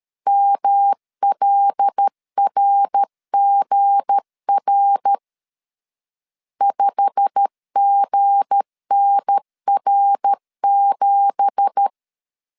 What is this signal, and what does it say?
Morse code, heard as a keyed tone, about 13 wpm, 800 Hz. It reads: MLRGR 5GNR7
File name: CW / DET / 1 9 1500C S A 12wpm 3800